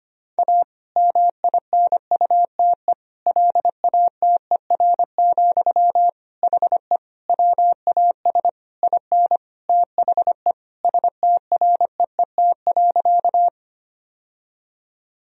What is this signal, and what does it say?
Morse code, heard as a keyed tone, about 25 wpm, 700 Hz. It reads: A MINUTE LATER, HE WAS IN THE STREET.
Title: Sentences from Top 1000 Words - Review Part 4 - 25wpm